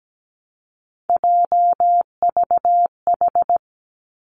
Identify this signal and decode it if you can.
Morse code, heard as a keyed tone, about 17 wpm, 700 Hz. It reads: JVH
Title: Three Challenging Letter-Number Combinations 17wpm